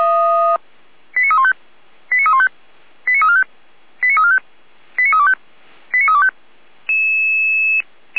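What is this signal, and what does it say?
Five-tone selective call (ZVEI), heard as quick runs of five tones, tones of about 80 ms. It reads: 89315 89315 89345 89345 89325 89325
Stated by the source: Five-tone ZVEI selective call received on an SDR.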